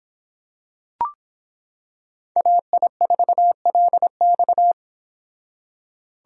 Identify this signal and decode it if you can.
Morse code, heard as a keyed tone, about 26 wpm, 700 Hz. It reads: AI4LX